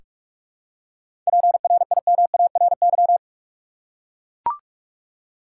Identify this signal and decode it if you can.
Morse code, heard as a keyed tone, about 43 wpm, 700 Hz. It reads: PRIMARY
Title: Seven Letter Words - 45wpm